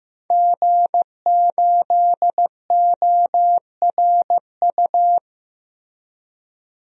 Morse code, heard as a keyed tone, about 15 wpm, 700 Hz. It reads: G8ORU
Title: DXCC-100 Call Signs - 5 Characters - Repeated 2x - 15wpm